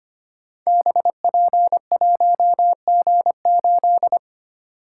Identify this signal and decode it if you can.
Morse code, heard as a keyed tone, about 25 wpm, 700 Hz. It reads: BP1G8
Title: Code Groups - 5 Letter:Numbers - 25wpm